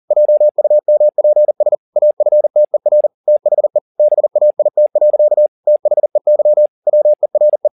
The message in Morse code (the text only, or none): JUMPS AFTER THE BAIT. THEY WERE